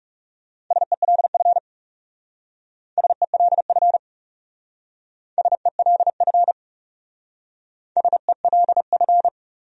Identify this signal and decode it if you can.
Morse code, heard as a keyed tone, about 36 wpm, 700 Hz. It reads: SELF SELF SELF SELF
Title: Speed-Racing Top 200 Words 30wpm